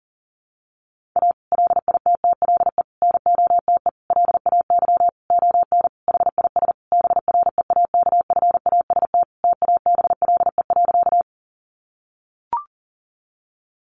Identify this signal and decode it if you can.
Morse code, heard as a keyed tone, about 40 wpm, 700 Hz. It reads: A LITTLE NOTE LAY ON HIS BREAKFAST TABLE.
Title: Sentences from Top 1500 Words 40wpm